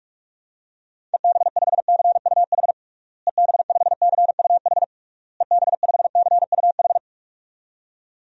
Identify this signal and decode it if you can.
Morse code, heard as a keyed tone, about 47 wpm, 700 Hz. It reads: EB5CUH EB5CUH EB5CUH
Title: DXCC-100 Call Signs - 6 Characters - Repeated 3x - 45wpm